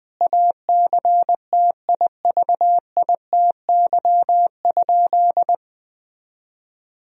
Morse code, heard as a keyed tone, about 20 wpm, 700 Hz. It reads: ACTIVITY?
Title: Sentences from Top 1000 Words - Review Part 4 - 20wpm